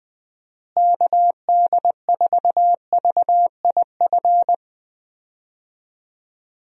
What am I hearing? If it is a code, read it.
Morse code, heard as a keyed tone, about 21 wpm, 700 Hz. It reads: KD4VIF
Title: USA Call Signs - 6 Characters - Repeated 3x - 20wpm